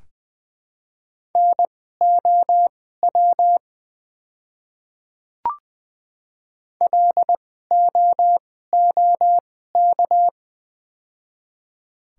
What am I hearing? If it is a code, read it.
Morse code, heard as a keyed tone, about 20 wpm, spaced as for 14 wpm, 700 Hz. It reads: NOW LOOK
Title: 2x Character Spacing - Top 100 Words - 20wpm